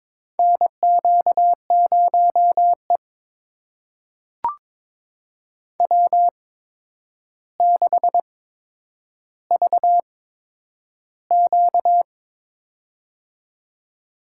Morse code, heard as a keyed tone, about 22 wpm, spaced as for 9 wpm, 700 Hz. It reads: NQ0E W6VQ
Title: USA Call Signs - 4 Characters - 8x Character Spacing - 22wpm